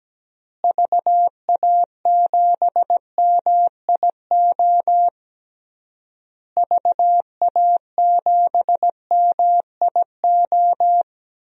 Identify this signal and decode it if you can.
Morse code, heard as a keyed tone, about 17 wpm, 700 Hz. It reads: VA7MIO VA7MIO
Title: DXCC-100 Call Signs - 6 Characters - Repeated 3x - 17wpm